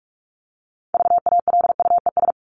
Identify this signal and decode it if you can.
Morse code, heard as a keyed tone, about 44 wpm, 700 Hz. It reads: VALUES
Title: ES:S Suffix - Top 100 Words with Base Words 45wpm